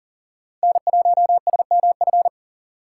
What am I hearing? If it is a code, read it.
Morse code, heard as a keyed tone, about 40 wpm, 700 Hz. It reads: N1SMF
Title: USA Call Signs - 5 Characters - 3x Character Spacing - 40wpm